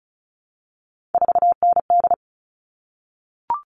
Morse code, heard as a keyed tone, about 34 wpm, 700 Hz. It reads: 4ND
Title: All Cut Numbers - 3-Digit 35wpm